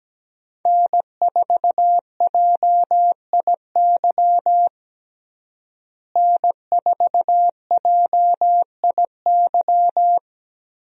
Morse code, heard as a keyed tone, about 17 wpm, 700 Hz. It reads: N4JIY N4JIY
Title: USA Call Signs - 5 Characters - Repeated 3x - 17wpm